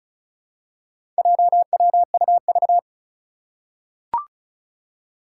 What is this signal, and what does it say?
Morse code, heard as a keyed tone, about 34 wpm, 700 Hz. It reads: JWUV
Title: Four Challenging Letter-Number Combinations 35wpm